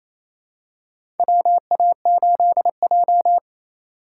Morse code, heard as a keyed tone, about 28 wpm, 700 Hz. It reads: WA8J
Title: USA Call Signs - 4x Character Spacing - 28wpm